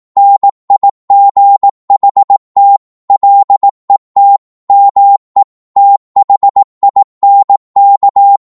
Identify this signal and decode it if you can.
Morse code, heard as a keyed tone, about 18 wpm, 800 Hz. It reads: NIGHT LET ME THINK
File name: Morse / psg-18wpm